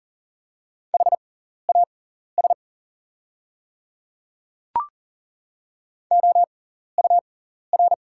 Morse code, heard as a keyed tone, about 40 wpm, spaced as for 14 wpm, 700 Hz. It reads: HAS OUR
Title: 6x Character Spacing - Top 200 Words - 40wpm